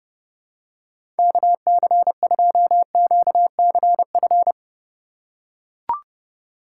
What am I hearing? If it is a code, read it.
Morse code, heard as a keyed tone, about 31 wpm, 700 Hz. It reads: KC2QCF
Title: USA Call Signs - 2x Character Spacing - 30wpm